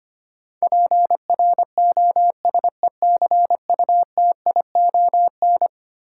Morse code, heard as a keyed tone, about 25 wpm, 700 Hz. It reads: PROSECUTION
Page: morse-code-ninja-practice-25wpm